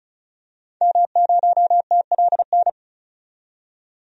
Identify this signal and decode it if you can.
Morse code, heard as a keyed tone, about 36 wpm, 700 Hz. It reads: M0TLN
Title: DXCC-100 Call Signs - 5 Characters - 35wpm